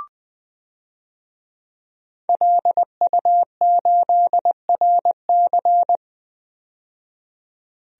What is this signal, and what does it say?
Morse code, heard as a keyed tone, about 20 wpm, 700 Hz. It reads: LU8RC
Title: DXCC-100 Call Signs - 5 Characters - 20wpm